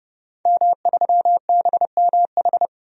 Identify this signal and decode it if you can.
Morse code, heard as a keyed tone, about 32 wpm, 700 Hz. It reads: M3BMH
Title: DXCC-100 Call Signs - 5 Characters - 30wpm